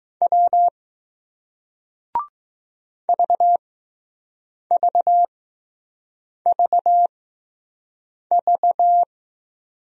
Morse code, heard as a keyed tone, about 21 wpm, spaced as for 6 wpm, 700 Hz. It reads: W VVVV